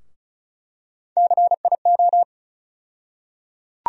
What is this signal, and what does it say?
Morse code, heard as a keyed tone, about 36 wpm, 700 Hz. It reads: CIO
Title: USA Call Sign Postfixes - 3 Letters 35wpm